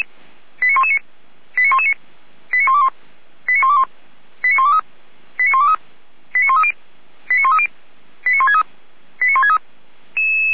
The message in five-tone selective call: 89119 89119 89121 89121 89123 89123 89130 89130 89163 89163

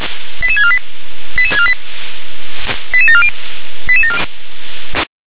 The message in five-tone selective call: 88648 88648 88740

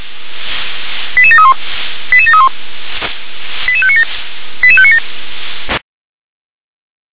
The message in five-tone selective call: 88731 88731 88597 88597

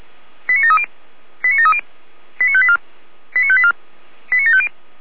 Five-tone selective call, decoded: 89730 89730 89584 89584 89750

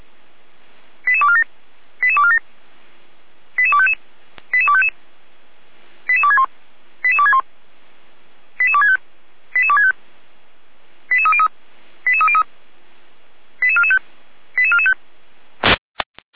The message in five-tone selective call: 88247 88247 88255 88255 88261 88261 88265 88265 88303 88303 88445 88445

